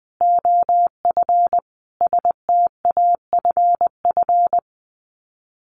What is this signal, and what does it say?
Morse code, heard as a keyed tone, about 20 wpm, 700 Hz. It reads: OF STAFF
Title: Sets of 5 Words Encore2 20wpm